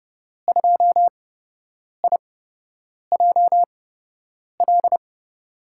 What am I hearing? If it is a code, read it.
Morse code, heard as a keyed tone, about 31 wpm, spaced as for 8 wpm, 700 Hz. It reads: 2IJL
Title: USA Call Signs - 5 Characters - 8x Character Spacing - 30wpm